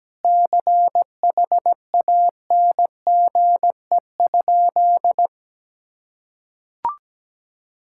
Morse code, heard as a keyed tone, about 17 wpm, 700 Hz. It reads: CHANGE?
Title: Sentences from Top 1600 Words 17wpm